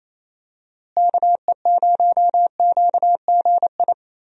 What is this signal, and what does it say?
Morse code, heard as a keyed tone, about 28 wpm, 700 Hz. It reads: KE0QGI
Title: USA Call Signs - 6 Characters - Repeated 2x - 28wpm